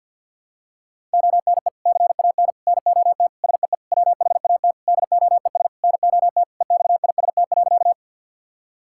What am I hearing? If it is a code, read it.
Morse code, heard as a keyed tone, about 49 wpm, 700 Hz. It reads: ONE CAN NOT SEE WHAT DOES NOT EXIST.
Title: Sentences from Top 1000 Words - Review Part 5 - 50wpm